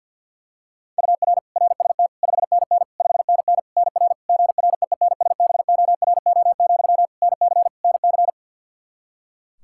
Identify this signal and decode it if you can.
Morse code, heard as a keyed tone, about 52 wpm, 700 Hz. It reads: UR RST 5NN 5NN NR GREENSBORO, NC NC
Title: Practice QSO - Exchange 1 - Beginning - 50wpm